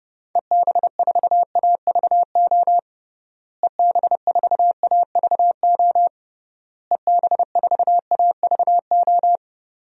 Morse code, heard as a keyed tone, about 29 wpm, 700 Hz. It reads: EB4AVO EB4AVO EB4AVO
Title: DXCC-100 Call Signs - 6 Characters - Repeated 3x - 30wpm